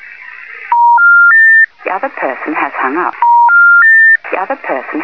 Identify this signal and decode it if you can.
Special information tone, with speech.